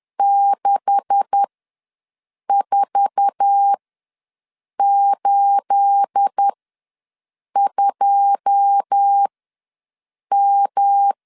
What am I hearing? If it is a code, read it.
Morse code, heard as a keyed tone, about 10 wpm, spaced as for 6 wpm, 800 Hz. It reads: 6482M
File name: LEZIONI / 9 286 ws 10wpm 4032